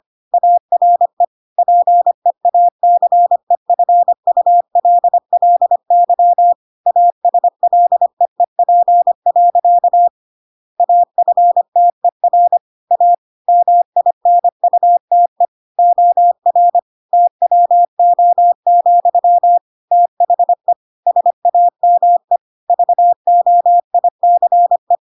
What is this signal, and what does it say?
Morse code, heard as a keyed tone, about 25 wpm, 700 Hz. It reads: ARE PEACEFULLY ASLEEP. AFTER A MINUTE OR TWO, THE SAME VOICE